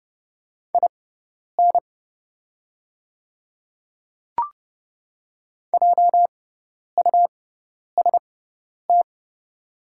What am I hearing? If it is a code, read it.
Morse code, heard as a keyed tone, about 30 wpm, spaced as for 10 wpm, 690 Hz. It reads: IN JUST
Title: 6x Character Spacing - Top 200 Words - 30wpm